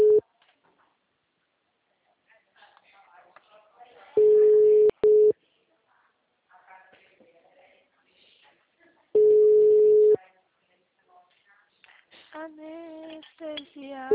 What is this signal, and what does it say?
Ringback tone (Europe, 425 Hz), with speech.